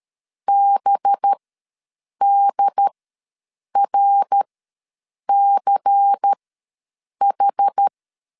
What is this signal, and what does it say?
Morse code, heard as a keyed tone, about 13 wpm, spaced as for 7 wpm, 800 Hz. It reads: BDRCH
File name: LEZIONI / 7 GF9Z / 7 G ws 12wpm 3140